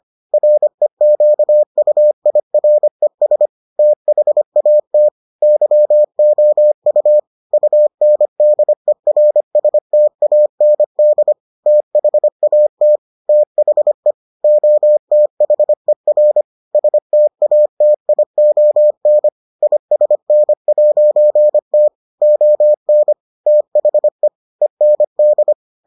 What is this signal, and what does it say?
Morse code, heard as a keyed tone, about 25 wpm, 600 Hz. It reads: REQUIRES THAT YOU UNDERSTAND THAT THE OTHER STATION ISN'T ON THE END